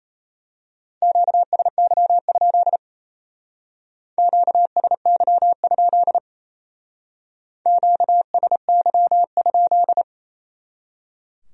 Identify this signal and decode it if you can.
Morse code, heard as a keyed tone, about 32 wpm, 700 Hz. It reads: QSY? QSY? QSY?
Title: Speed-Racing Q-Codes - 28wpm